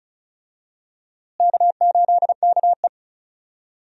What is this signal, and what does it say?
Morse code, heard as a keyed tone, about 36 wpm, 700 Hz. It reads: K8KE
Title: USA Call Signs - 4 Characters - Repeated 3x - 35wpm